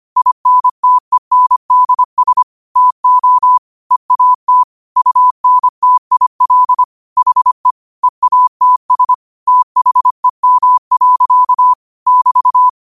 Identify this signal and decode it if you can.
Morse code, heard as a keyed tone, about 25 wpm, 1000 Hz. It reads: INTENDS TO EAT UNTIL HE EATS THEM. <BT>
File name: qotdfiles / 25wpm